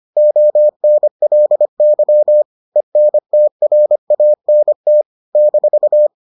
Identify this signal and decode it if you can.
Morse code, heard as a keyed tone, about 25 wpm, 600 Hz. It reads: ONLY ENTRANT -